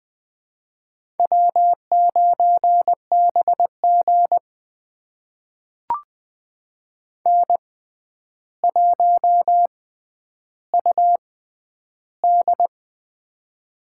Morse code, heard as a keyed tone, about 20 wpm, spaced as for 10 wpm, 700 Hz. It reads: W9BG N1UD
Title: USA Call Signs - 4 Characters - 6x Character Spacing - 20wpm